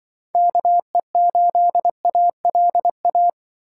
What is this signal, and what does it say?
Morse code, heard as a keyed tone, about 24 wpm, 700 Hz. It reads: KE8ALA
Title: Speed Racing - USA 2x3 Call Signs 20wpm